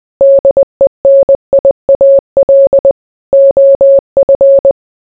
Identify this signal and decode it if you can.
Morse code, heard as a keyed tone, about 20 wpm, 550 Hz. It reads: DENIAL OF